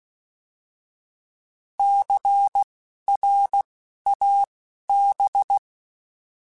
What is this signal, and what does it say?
Morse code, heard as a keyed tone, about 16 wpm, spaced as for 12 wpm, 800 Hz. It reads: CRAB
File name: Alphabetical / Alphabetical mp3s / Cs @10